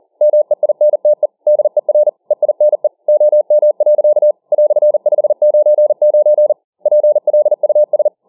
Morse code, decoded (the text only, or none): MEINTE DER EINE OM. "599 PLUS